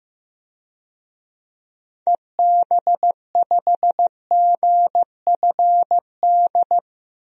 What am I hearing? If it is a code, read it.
Morse code, heard as a keyed tone, about 15 wpm, 700 Hz. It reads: EB5GFD
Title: DXCC-100 Call Signs - 6 Characters - 15wpm